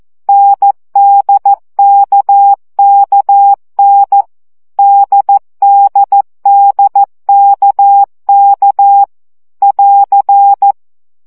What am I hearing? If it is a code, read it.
Morse code, heard as a keyed tone, about 14 wpm, 800 Hz. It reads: NDKKN DDDKK <AR>
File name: CW gyakorlas n d k 12 baud